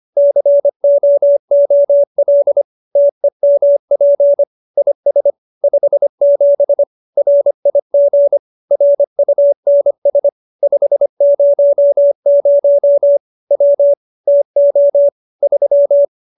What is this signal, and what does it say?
Morse code, heard as a keyed tone, about 25 wpm, 570 Hz. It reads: COOL TEMP IS 57 RIG RUNS 500 W TO 3
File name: qso201b 25